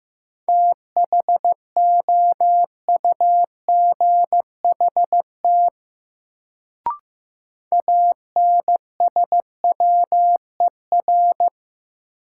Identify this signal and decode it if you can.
Morse code, heard as a keyed tone, about 15 wpm, 700 Hz. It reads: THOUGHT ANSWER
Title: Sentences from Top 300 Words - New Word Drill - 15wpm